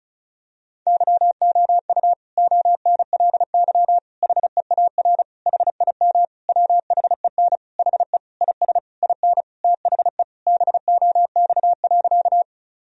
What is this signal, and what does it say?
Morse code, heard as a keyed tone, about 36 wpm, 700 Hz. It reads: YOU ONLY HEAR HIM WHEN HE IS IN THE BOX.
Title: Sentences from Top 1000 Words - Review Part 2 - 35wpm